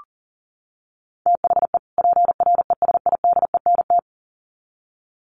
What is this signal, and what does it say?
Morse code, heard as a keyed tone, about 40 wpm, 700 Hz. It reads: THE PRESIDENT
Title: Sets of 2 Words 40wpm